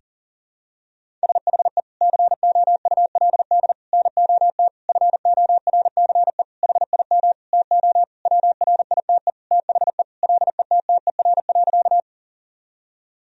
Morse code, heard as a keyed tone, about 40 wpm, 700 Hz. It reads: SHE COULD NOT FORCE HIM TO WRITE THE LETTER.
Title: Sentences from Top 1000 Words - Review Part 3 - 40wpm